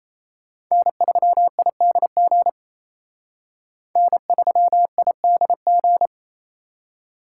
Morse code, heard as a keyed tone, about 35 wpm, 700 Hz. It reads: N3IDG N3IDG
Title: Speed Racing - USA 1x3 Call Signs 28wpm